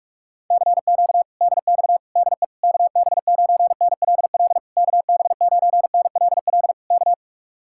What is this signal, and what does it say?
Morse code, heard as a keyed tone, about 40 wpm, 700 Hz. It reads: CQ DX DE KB9NLL KB9NLL K